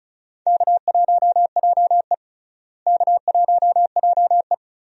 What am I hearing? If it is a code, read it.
Morse code, heard as a keyed tone, about 36 wpm, 700 Hz. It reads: K1JE K1JE
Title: USA Call Signs - 4 Characters - Repeated 3x - 35wpm